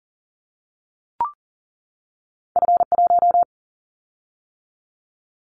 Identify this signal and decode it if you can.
Morse code, heard as a keyed tone, about 40 wpm, 700 Hz. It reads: F1
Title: Callsign Prefixes for Top 25 DXCC - 40wpm